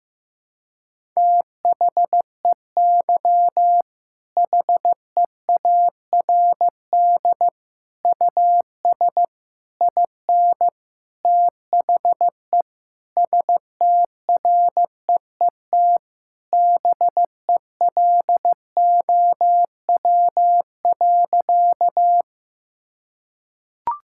Morse code, heard as a keyed tone, about 15 wpm, 700 Hz. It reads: THEY HEARD US IN THE STREET BELOW.
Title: Sentences from Top 1000 Words - Review Part 5 - 15wpm